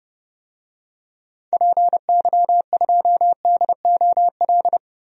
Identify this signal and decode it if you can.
Morse code, heard as a keyed tone, about 30 wpm, 700 Hz. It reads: PY2DOL
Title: DXCC-100 Call Signs - 6 Characters - Repeated 2x - 30wpm